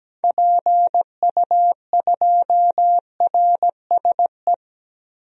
Morse code, heard as a keyed tone, about 17 wpm, 700 Hz. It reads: PU2RSE
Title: DXCC-100 Call Signs - 6 Characters - 17wpm